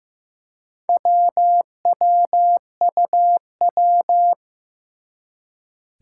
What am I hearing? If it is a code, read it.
Morse code, heard as a keyed tone, about 15 wpm, 700 Hz. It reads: WWUW
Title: Four Challenging Letter-Number Combinations 15wpm